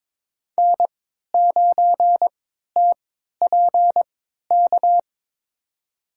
Morse code, heard as a keyed tone, about 22 wpm, spaced as for 13 wpm, 700 Hz. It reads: N9TPK
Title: USA Call Signs - 5 Characters - 3x Character Spacing - 22wpm